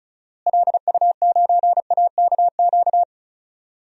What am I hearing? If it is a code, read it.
Morse code, heard as a keyed tone, about 36 wpm, 700 Hz. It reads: LU9AKQ
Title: DXCC-100 Call Signs - 6 Characters - Repeated 3x - 35wpm